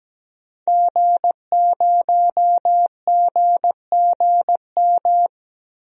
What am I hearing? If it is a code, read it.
Morse code, heard as a keyed tone, about 17 wpm, 700 Hz. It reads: G0GGM
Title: DXCC-100 Call Signs - 5 Characters - Repeated 3x - 17wpm